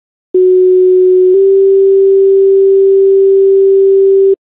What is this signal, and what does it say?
Two-tone page, heard as a one-second tone, then a three-second tone, about 370 then 390 Hz. It is heard from 0.3 to 4.4 s.